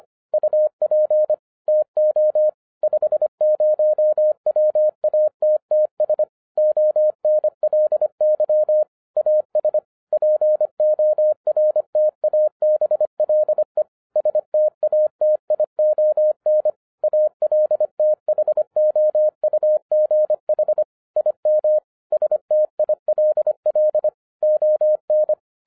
Morse code, heard as a keyed tone, about 25 wpm, 610 Hz. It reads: UP TO 50 WATTS ONLY AS PORTABLE STATION ALTHOUGH IM STILL ON
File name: 2012-01-mp3 / n08-ITS MY FAULT, BUT-25wpm